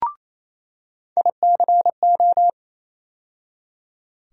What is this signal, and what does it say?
Morse code, heard as a keyed tone, about 28 wpm, 700 Hz. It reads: ICO
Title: USA Call Sign Postfixes - 3 Letters 28wpm